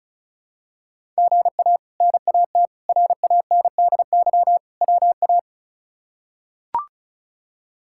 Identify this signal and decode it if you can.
Morse code, heard as a keyed tone, about 34 wpm, 700 Hz. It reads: GA NAT RANDY WA
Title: K1USN SST QSO Practice - 35wpm